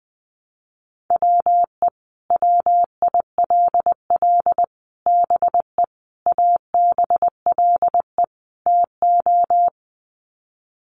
Morse code, heard as a keyed tone, about 20 wpm, 700 Hz. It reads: WE WILL BE ABLE TO
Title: Sets of 5 Words Encore 20wpm